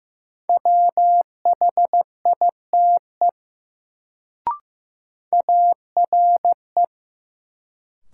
Morse code, heard as a keyed tone, about 15 wpm, 700 Hz. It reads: WHITE ARE
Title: Sets of 2 Words - New Word Drill - 15wpm